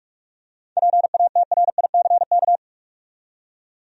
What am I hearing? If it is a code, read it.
Morse code, heard as a keyed tone, about 46 wpm, 700 Hz. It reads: PATRICK
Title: USA Common Names - 45wpm